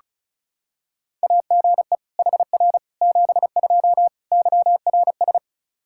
Morse code, heard as a keyed tone, about 34 wpm, 700 Hz. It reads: AGE HR 72 YRS
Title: K1USN SST QSO Practice 35wpm